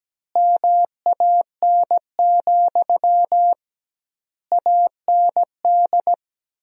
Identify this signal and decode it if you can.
Morse code, heard as a keyed tone, about 17 wpm, 700 Hz. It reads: MAN, AND
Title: 2x Word-Spacing - Sentences from Top 200 Words - 17wpm